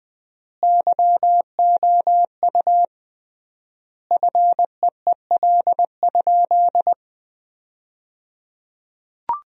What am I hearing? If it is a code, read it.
Morse code, heard as a keyed tone, about 20 wpm, 700 Hz. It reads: YOU FEEL?
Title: 3x Word-Spacing - Sentences from Top 400 Words - 20wpm